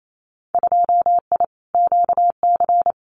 Morse code, heard as a keyed tone, about 28 wpm, 700 Hz. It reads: 2I QC
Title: Winter Field Day Exchanges 28wpm